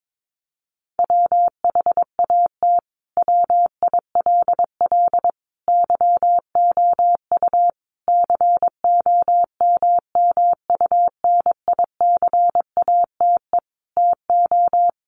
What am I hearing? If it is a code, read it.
Morse code, heard as a keyed tone, about 22 wpm, 700 Hz. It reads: WHAT WILL YOU COMMUNICATE TO
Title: Sentences from Top 1300 Words 22wpm